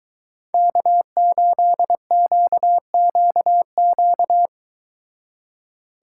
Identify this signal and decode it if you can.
Morse code, heard as a keyed tone, about 22 wpm, 700 Hz. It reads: K8QQQ